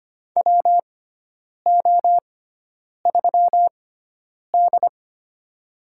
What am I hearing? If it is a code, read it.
Morse code, heard as a keyed tone, about 25 wpm, spaced as for 9 wpm, 700 Hz. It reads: WO3D
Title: USA Call Signs - 6x Character Spacing - 25wpm